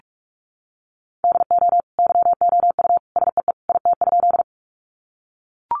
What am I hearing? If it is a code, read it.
Morse code, heard as a keyed tone, about 44 wpm, 700 Hz. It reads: DO YOU SEE IT?
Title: Sentences from Top 100 Words 45wpm